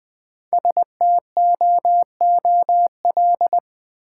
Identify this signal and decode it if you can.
Morse code, heard as a keyed tone, about 20 wpm, 700 Hz. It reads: STOOL
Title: Easy Words with Hints - 20wpm